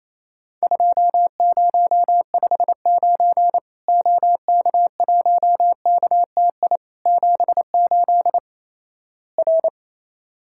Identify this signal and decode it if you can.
Morse code, heard as a keyed tone, about 28 wpm, 700 Hz. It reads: 2059 OK1KTI 78 R
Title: Worked All Europe DX Contest - QTC Exchange - 28wpm